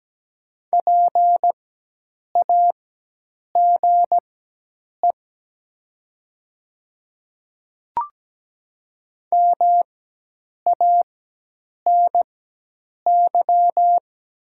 Morse code, heard as a keyed tone, about 17 wpm, spaced as for 8 wpm, 700 Hz. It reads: PAGE MANY
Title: 4x Character Spacing - Top 300 Words - 17wpm